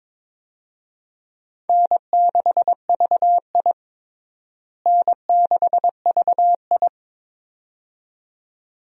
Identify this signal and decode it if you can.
Morse code, heard as a keyed tone, about 22 wpm, 700 Hz. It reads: N6VI N6VI
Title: USA Call Signs - 4 Characters - Repeated 2x - 22wpm